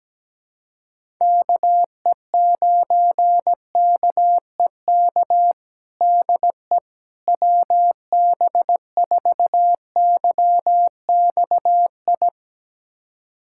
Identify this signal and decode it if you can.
Morse code, heard as a keyed tone, about 17 wpm, 700 Hz. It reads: KE9KEK DE WB4YXI
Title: USA Callsign DE Callsign - 6 Characters - 17wpm